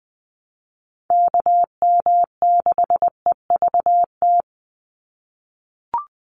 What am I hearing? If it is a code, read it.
Morse code, heard as a keyed tone, about 20 wpm, 700 Hz. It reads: KM6EVT
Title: USA Call Signs 20wpm